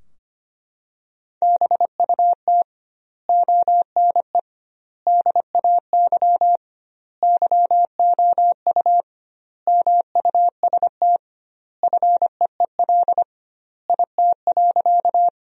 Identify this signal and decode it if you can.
Morse code, heard as a keyed tone, about 25 wpm, 700 Hz. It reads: BUT ONE DAY YOU MUST FEEL IT.
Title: 2x Word-Spacing - Sentences from Top 400 Words - 25wpm